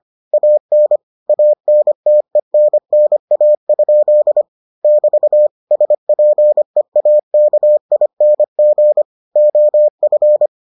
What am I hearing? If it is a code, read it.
Morse code, heard as a keyed tone, about 25 wpm, 600 Hz. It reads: AN ANTENNA? <BT> SPEAKING OF